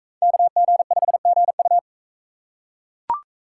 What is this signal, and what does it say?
Morse code, heard as a keyed tone, about 38 wpm, 700 Hz. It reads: KG5GU